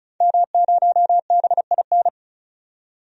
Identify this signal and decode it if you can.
Morse code, heard as a keyed tone, about 36 wpm, 700 Hz. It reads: M0BIN
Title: DXCC-100 Call Signs - 5 Characters - 35wpm